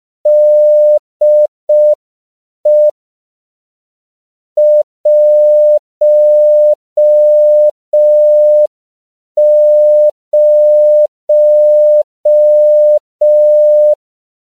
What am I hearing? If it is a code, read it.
Morse code, heard as a keyed tone, about 5 wpm, 600 Hz. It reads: DE 10
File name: CW / 5PPM